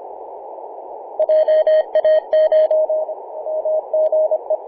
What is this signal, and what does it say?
Morse code, heard as a keyed tone, about 25 wpm, 620 Hz. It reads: JACH MGE
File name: Apr/29 0606z VK9NT 50.115MHz CW